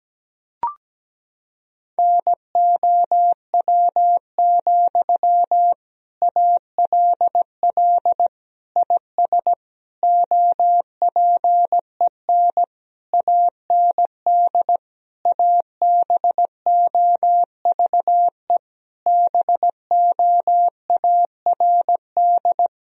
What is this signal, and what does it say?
Morse code, heard as a keyed tone, about 17 wpm, 700 Hz. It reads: NOW, ALL IS OPEN AND ABOVE BOARD
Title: Sentences from Top 1000 Words - Review Part 4 - 17wpm